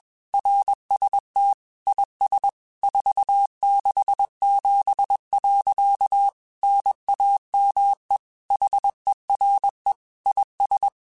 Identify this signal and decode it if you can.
Morse code, heard as a keyed tone, about 21 wpm, 800 Hz. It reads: RST IS 467. NAME HERE IS
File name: QSOs / QSOs-mp3s / kc7haa@20